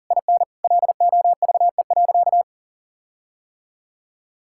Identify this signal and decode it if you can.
Morse code, heard as a keyed tone, about 37 wpm, 700 Hz. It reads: IN LOVE.
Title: Sentences from Top 1600 Words 40wpm